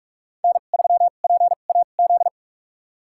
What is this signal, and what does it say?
Morse code, heard as a keyed tone, about 47 wpm, spaced as for 33 wpm, 700 Hz. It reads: N3PAZ
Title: USA Call Signs - 5 Characters - 2x Character Spacing - 45wpm